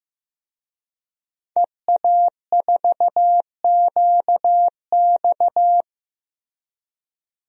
Morse code, heard as a keyed tone, about 16 wpm, 700 Hz. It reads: EA4QX
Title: DXCC-100 Call Signs - 5 Characters - Repeated 3x - 15wpm